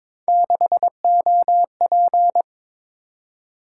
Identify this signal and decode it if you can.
Morse code, heard as a keyed tone, about 22 wpm, 700 Hz. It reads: 6OP
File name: practice / 22wpm / Three Letter-Number Combinations - 22wpm